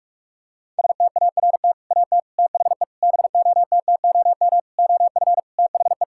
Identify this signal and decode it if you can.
Morse code, heard as a keyed tone, about 42 wpm, 700 Hz. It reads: START AT THE BOTTOM OF THE
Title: Sentences from Top 1000 Words - Review Part 6 - 45wpm